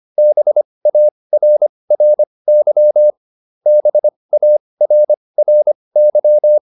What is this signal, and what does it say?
Morse code, heard as a keyed tone, about 25 wpm, spaced as for 20 wpm, 600 Hz. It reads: BARRY BARRY